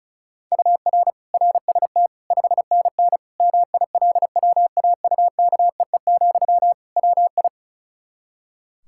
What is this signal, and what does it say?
Morse code, heard as a keyed tone, about 36 wpm, 700 Hz. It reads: UR RST 5NN MILWAUKEE, WI
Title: K1USN SST QSO Practice 35wpm